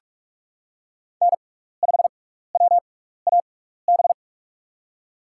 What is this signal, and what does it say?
Morse code, heard as a keyed tone, about 41 wpm, spaced as for 15 wpm, 700 Hz. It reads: N5WAB
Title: USA Call Signs - 6x Character Spacing - 45wpm